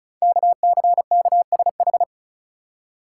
Morse code, heard as a keyed tone, about 36 wpm, 690 Hz. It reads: KCKSH
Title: Code Groups - 5 Letters - 35wpm